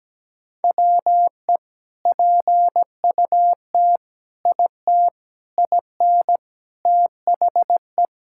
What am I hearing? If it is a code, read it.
Morse code, heard as a keyed tone, about 17 wpm, 700 Hz. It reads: WE PUT IT IN THE
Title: Sentences from Top 1000 Words - Review Part 5 - 17wpm